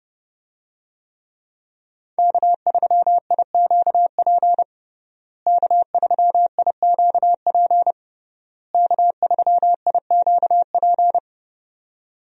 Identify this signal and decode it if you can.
Morse code, heard as a keyed tone, about 32 wpm, 700 Hz. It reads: K3IQP K3IQP K3IQP
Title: USA Call Signs - 5 Characters - Repeated 3x - 30wpm